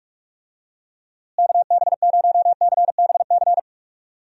Morse code, heard as a keyed tone, about 46 wpm, 700 Hz. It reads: KB0CBC
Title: USA Call Signs - 6 Characters - Repeated 2x - 45wpm